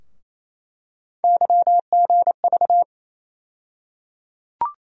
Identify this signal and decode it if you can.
Morse code, heard as a keyed tone, about 28 wpm, 700 Hz. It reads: YGV
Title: USA Call Sign Postfixes - 3 Letters 28wpm